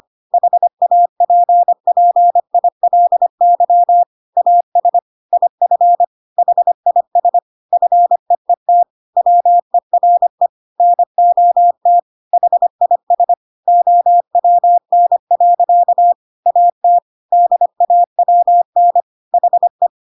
Morse code, heard as a keyed tone, about 25 wpm, 700 Hz. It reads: HAPPILY AS IF HIS FEET WERE NOT HIS OWN. AT DAWN HE